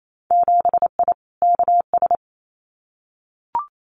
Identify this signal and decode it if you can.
Morse code, heard as a keyed tone, about 28 wpm, 700 Hz. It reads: 7I KS
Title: Winter Field Day Exchanges Encore 28wpm